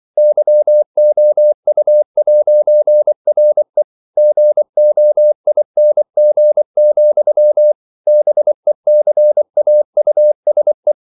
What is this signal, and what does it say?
Morse code, heard as a keyed tone, about 24 wpm, 600 Hz. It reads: YOU'RE GOING, BECAUSE
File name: CW / Practice / EnglishQuotations0324